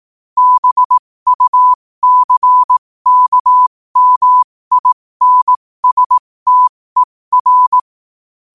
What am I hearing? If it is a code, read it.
Morse code, heard as a keyed tone, about 18 wpm, spaced as for 15 wpm, 1000 Hz. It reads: BUCKMINSTER
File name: qotdfiles / 15wpm